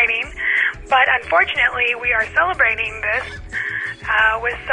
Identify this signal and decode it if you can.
Emergency Alert System (SAME) tones, heard as harsh buzzing bursts, with speech. It is heard from 0.3 to 4.4 s.